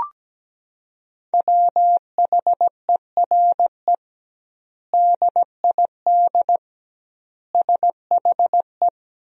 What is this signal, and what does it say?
Morse code, heard as a keyed tone, about 17 wpm, 700 Hz. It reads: WHERE DID SHE
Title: 2x Word-Spacing - Sentences from Top 200 Words - 17wpm